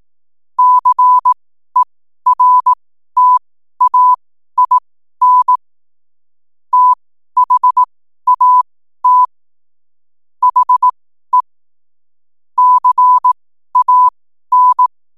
Morse code, heard as a keyed tone, about 18 wpm, spaced as for 12 wpm, 1000 Hz. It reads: CERTAIN THAT HE CAN
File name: n7net / 12wpm